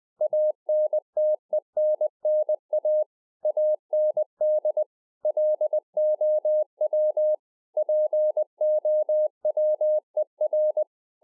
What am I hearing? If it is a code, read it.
Morse code, heard as a keyed tone, about 20 wpm, 610 Hz. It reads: ANTENNA AND LOW POWER